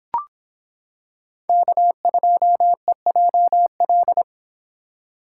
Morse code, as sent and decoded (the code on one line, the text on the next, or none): -.- ..--- . .--- .-..
K2EJL